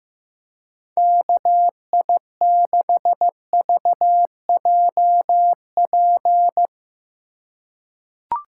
Morse code, heard as a keyed tone, about 15 wpm, 700 Hz. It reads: KI6VJP
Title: USA Call Signs - 6 Characters - 2x Character Spacing - 15wpm